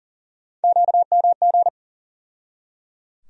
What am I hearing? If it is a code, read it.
Morse code, heard as a keyed tone, about 37 wpm, 700 Hz. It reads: QMG
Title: USA Call Sign Postfixes - 3 Letters 40wpm